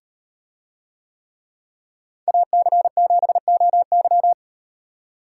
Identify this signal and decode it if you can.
Morse code, heard as a keyed tone, about 38 wpm, 700 Hz. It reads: AC7OY